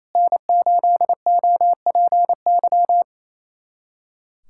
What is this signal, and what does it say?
Morse code, heard as a keyed tone, about 28 wpm, 700 Hz. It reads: N8OPY
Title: USA Callsign DE Callsign - 5 Characters - 28wpm